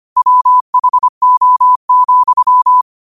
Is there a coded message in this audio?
Yes: Morse code from 0.2 to 2.8 s, heard as a keyed tone, about 25 wpm, 1000 Hz. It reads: WHO,